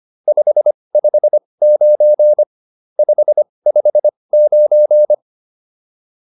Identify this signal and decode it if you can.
Morse code, heard as a keyed tone, about 25 wpm, spaced as for 20 wpm, 600 Hz. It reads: 559 559